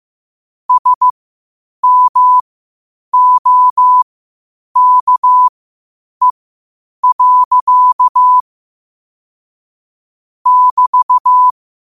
Morse code, heard as a keyed tone, about 15 wpm, spaced as for 8 wpm, 1000 Hz. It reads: SMOKE. <BT>
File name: qotdfiles / 8wpm